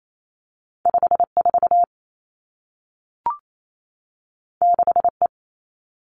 Morse code, heard as a keyed tone, about 28 wpm, 700 Hz. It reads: 5 4 6 E
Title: Two Challenging Letter-Number Combinations 28wpm